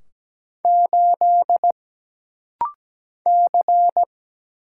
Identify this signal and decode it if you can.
Morse code, heard as a keyed tone, about 17 wpm, 700 Hz. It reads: 8 C
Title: Single Letter-Number Alt - Rapid-Fire 17wpm